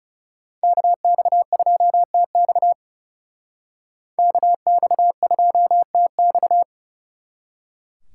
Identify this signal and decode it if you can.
Morse code, heard as a keyed tone, about 39 wpm, 700 Hz. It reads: KX2TX KX2TX